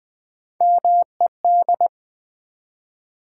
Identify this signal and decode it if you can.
Morse code, heard as a keyed tone, about 20 wpm, 700 Hz. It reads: MED
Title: Build A Word - 1 Letter At A Time - 5-Letter Words - 20wpm